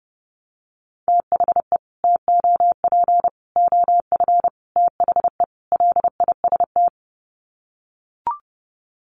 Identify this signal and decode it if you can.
Morse code, heard as a keyed tone, about 30 wpm, 700 Hz. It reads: THE TOP OF THE LIST
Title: Sets of 5 Words Encore2 30wpm